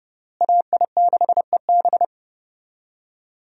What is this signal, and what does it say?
Morse code, heard as a keyed tone, about 32 wpm, 700 Hz. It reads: AI6EB
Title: USA Call Signs - 5 Characters - Repeated 3x - 30wpm